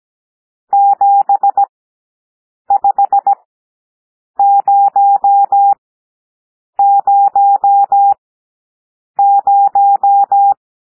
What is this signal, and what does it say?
Morse code, heard as a keyed tone, about 17 wpm, spaced as for 7 wpm, 800 Hz. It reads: 75000
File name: SOUND NUMBERS 17 wpm